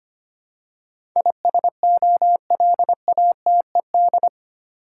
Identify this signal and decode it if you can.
Morse code, heard as a keyed tone, about 25 wpm, 700 Hz. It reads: ISOLATED
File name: 8 Letter Words - First 3 Letters Given - 25wpm